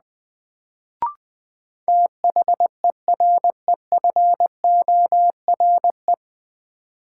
Morse code, heard as a keyed tone, about 20 wpm, 700 Hz. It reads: THEREFORE
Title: Sentences from Top 800 Words - New Word Drill - 20wpm